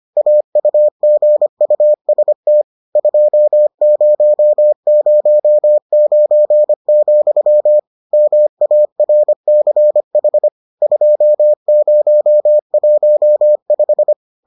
Morse code, heard as a keyed tone, about 25 wpm, 600 Hz. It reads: AUGUST 2009, MARCH 2015